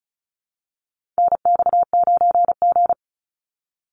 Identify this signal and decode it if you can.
Morse code, heard as a keyed tone, about 34 wpm, 700 Hz. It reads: NX9G
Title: Call Signs Repeated 3x - 35wpm